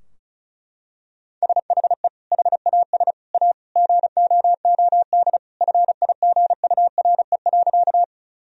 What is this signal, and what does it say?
Morse code, heard as a keyed tone, about 36 wpm, 700 Hz. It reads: SHE HAS A GOOD FIGURE.
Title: Sentences from Top 1000 Words - Review Part 2 - 35wpm